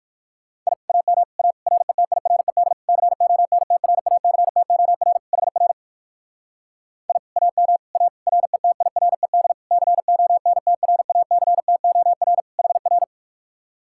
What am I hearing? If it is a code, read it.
Morse code, heard as a keyed tone, about 54 wpm, 700 Hz. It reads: I AM A RETIRED CONTRACTOR HR I AM A RETIRED CONTRACTOR HR
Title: Speed-Racing Common QSO Phrases - 45wpm